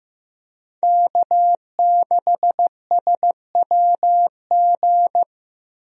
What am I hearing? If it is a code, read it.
Morse code, heard as a keyed tone, about 15 wpm, 700 Hz. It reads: K6SWG
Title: USA Call Signs - 5 Characters - Repeated 3x - 15wpm